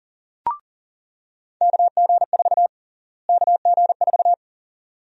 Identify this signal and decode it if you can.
Morse code, heard as a keyed tone, about 40 wpm, 700 Hz. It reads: KG4 KG4
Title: USA Call Sign Prefixes - Repeated 2x - 40wpm